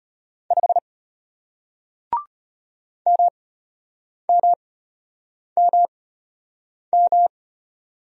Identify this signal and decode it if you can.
Morse code, heard as a keyed tone, about 34 wpm, spaced as for 8 wpm, 690 Hz. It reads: 5 MMMM